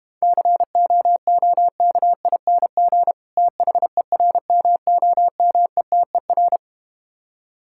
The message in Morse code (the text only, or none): COOKING THERMOMETER